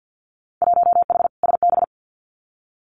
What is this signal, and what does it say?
Morse code, heard as a keyed tone, about 50 wpm, 700 Hz. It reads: 1H SB
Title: Winter Field Day Exchanges 50wpm